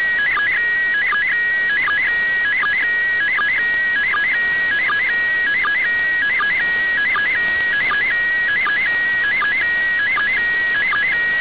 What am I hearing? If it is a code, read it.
SSTV (slow-scan television), PD 180.
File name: suws-websdr recording t18-13-19z -8khz